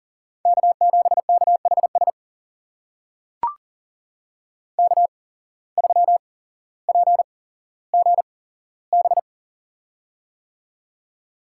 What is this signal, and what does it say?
Morse code, heard as a keyed tone, about 40 wpm, spaced as for 17 wpm, 700 Hz. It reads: K7KHS K3PGB
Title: USA Call Signs - 5 Characters - 8x Character Spacing - 40wpm